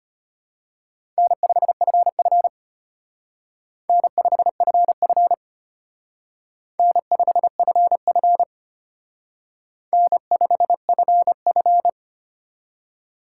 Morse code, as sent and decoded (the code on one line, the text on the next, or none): -. ..... ..-. ..-. / -. ..... ..-. ..-. / -. ..... ..-. ..-. / -. ..... ..-. ..-.
N5FF N5FF N5FF N5FF